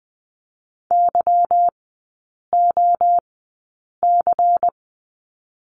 Morse code, heard as a keyed tone, about 20 wpm, spaced as for 8 wpm, 700 Hz. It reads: YOC
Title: 205 Lesson 33e - Letter j - Three Characters 20wpm